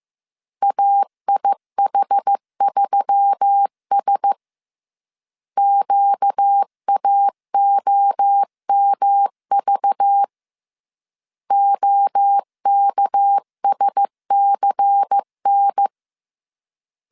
Morse code, heard as a keyed tone, about 15 wpm, 800 Hz. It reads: AIH3S QAOMV OKSCN